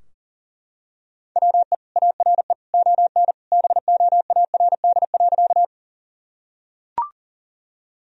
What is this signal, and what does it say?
Morse code, heard as a keyed tone, about 37 wpm, 700 Hz. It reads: WE ARE ON BOARD.
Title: Sentences from Top 1000 Words - Review Part 4 - 40wpm